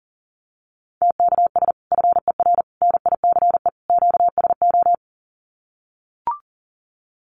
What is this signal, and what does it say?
Morse code, heard as a keyed tone, about 40 wpm, 700 Hz. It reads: TKS FER NICE QSO
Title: Common QSO Phrases 40wpm